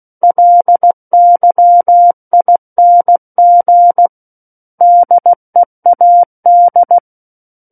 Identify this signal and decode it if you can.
Morse code, heard as a keyed tone, about 16 wpm, 700 Hz. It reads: LYING DEAD
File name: Book One / 07-Chapter 7 - 16 WPM